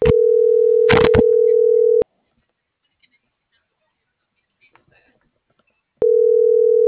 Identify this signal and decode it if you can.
Ringback tone (North America), 440 + 480 Hz.